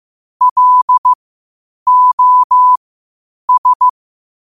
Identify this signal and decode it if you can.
Morse code, heard as a keyed tone, about 15 wpm, spaced as for 8 wpm, 1000 Hz. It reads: LOS